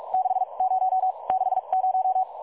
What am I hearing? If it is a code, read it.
Morse code, heard as a keyed tone, about 43 wpm, spaced as for 33 wpm, 750 Hz. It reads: 60 60
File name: dlazaawansowanych / nagraniazpasma / websdr recording 3565.0kHz